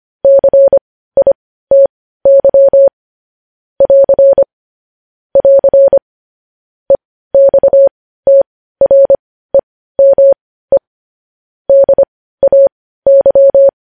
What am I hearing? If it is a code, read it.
Morse code, heard as a keyed tone, about 25 wpm, spaced as for 15 wpm, 550 Hz. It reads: CITY <AR> <AR> EXTREME DAY
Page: news headlines in morse code at 15 wpm